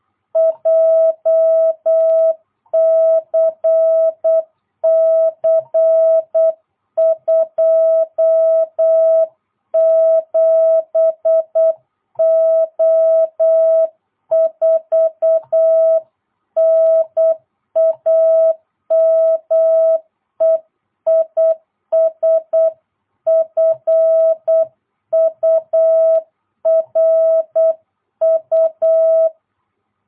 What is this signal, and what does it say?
Morse code, heard as a keyed tone, about 8 wpm, 660 Hz. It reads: JCC 27O4 NAME IS FURU